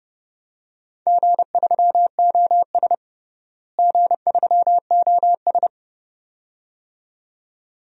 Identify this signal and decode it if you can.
Morse code, heard as a keyed tone, about 32 wpm, 700 Hz. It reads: G3OS G3OS
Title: DXCC-100 Call Signs - 4 Characters - Repeated 2x - 30wpm